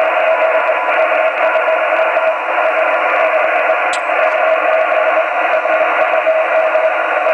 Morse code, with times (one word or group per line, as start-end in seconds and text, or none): none